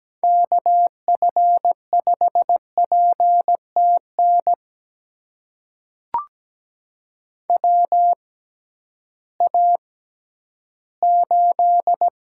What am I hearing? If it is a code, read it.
Morse code, heard as a keyed tone, about 17 wpm, spaced as for 11 wpm, 700 Hz. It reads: KF5PTN WA8
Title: USA Call Signs - 6x Character Spacing - 17wpm